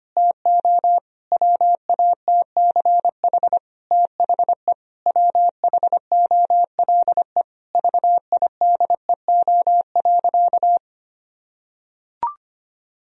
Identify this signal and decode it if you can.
Morse code, heard as a keyed tone, about 25 wpm, 700 Hz. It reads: TO WATCH THE WHOLE VIDEO.
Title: Sentences from Top 1000 Words - Review Part 4 - 25wpm